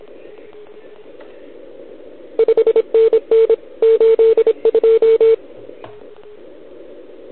Morse code, heard as a keyed tone, about 27 wpm, 450 Hz. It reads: 5NN 82
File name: ARCHIV / OKDX2020 / OL4Wa